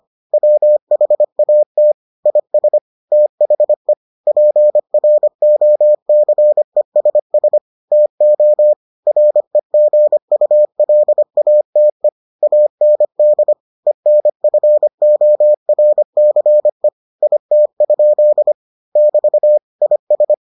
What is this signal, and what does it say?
Morse code, heard as a keyed tone, about 25 wpm, 600 Hz. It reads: WHAT IS THE PROCESS TO REGULATE AND ENFORCE IT? <BT> IS